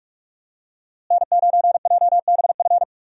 Morse code, heard as a keyed tone, about 42 wpm, 700 Hz. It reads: N9JBF
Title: USA Call Signs - 5 Characters - Repeated 2x - 45wpm